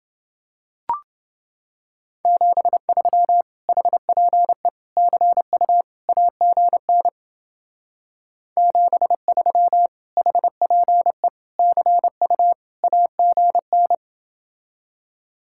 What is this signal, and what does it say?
Morse code, heard as a keyed tone, about 28 wpm, 700 Hz. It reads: 73 HPE CU AGN 73 HPE CU AGN